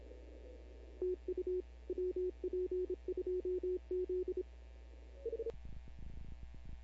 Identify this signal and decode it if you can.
Morse code, heard as a keyed tone, about 26 wpm, 370 Hz. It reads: TU WP2Z
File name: CW / 2005-k5tr arrl cw dx -15:00